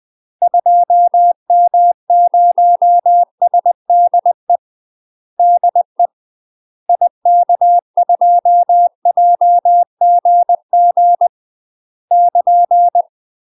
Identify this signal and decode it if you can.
Morse code, heard as a keyed tone, about 20 wpm, 700 Hz. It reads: 2M0SDE DE IK2JGG <KN>
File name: UK MASTER QSOs TRAINING / 20WPM / UK MASTER QSOs 0063